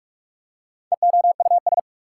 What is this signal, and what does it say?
Morse code, heard as a keyed tone, about 46 wpm, 690 Hz. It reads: EOUS
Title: Prefixes and Suffixes 45wpm